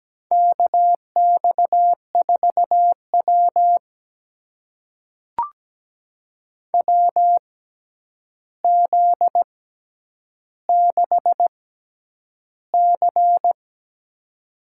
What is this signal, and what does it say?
Morse code, heard as a keyed tone, about 17 wpm, spaced as for 9 wpm, 700 Hz. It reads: KX4W WZ6C
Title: USA Call Signs - 4 Characters - 6x Character Spacing - 17wpm